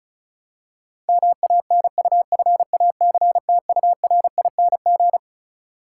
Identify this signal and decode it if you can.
Morse code, heard as a keyed tone, about 36 wpm, 700 Hz. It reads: MANUFACTURING
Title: Practice QSO - Exchange 3 - More Info - 35wpm